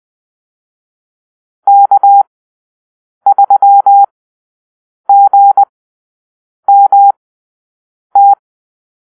Morse code, heard as a keyed tone, about 20 wpm, spaced as for 7 wpm, 800 Hz. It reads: K3GMT
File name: arrl / Call-Sign Drill-9